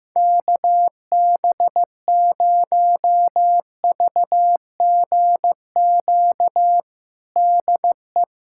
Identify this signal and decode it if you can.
Morse code, heard as a keyed tone, about 15 wpm, 700 Hz. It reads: KB0VGQ DE
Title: USA Callsign DE Callsign 15wpm